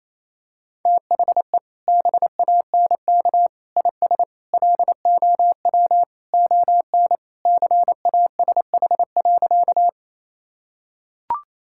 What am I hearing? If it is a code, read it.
Morse code, heard as a keyed tone, about 28 wpm, 700 Hz. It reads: THE BANK IS LOW ON CASH.
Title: Sentences from Top 1000 Words - Review Part 5 - 28wpm